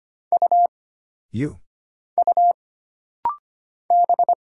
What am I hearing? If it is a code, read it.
Morse code, heard as a keyed tone, about 25 wpm, 700 Hz. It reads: U U B
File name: Single Character (Letter:Number:Punctuation:Prosign) - Rapid-Fire - 25wpm